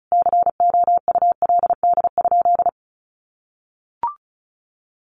Morse code, heard as a keyed tone, about 34 wpm, 700 Hz. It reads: COULD?
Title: Sentences from Top 400 Words 35wpm